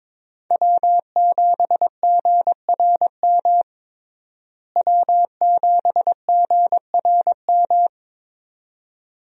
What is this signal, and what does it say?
Morse code, heard as a keyed tone, about 22 wpm, 700 Hz. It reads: W7GRM W7GRM
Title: USA Call Signs - 5 Characters - Repeated 2x - 22wpm